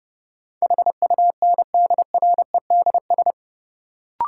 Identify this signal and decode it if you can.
Morse code, heard as a keyed tone, about 31 wpm, 700 Hz. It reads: HUNDREDS
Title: Sets of 4 Words - New Word Drill - 30wpm